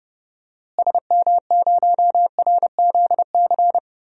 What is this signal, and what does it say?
Morse code, heard as a keyed tone, about 32 wpm, 700 Hz. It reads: SM0RZC
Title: DXCC-100 Call Signs - 6 Characters - Repeated 2x - 30wpm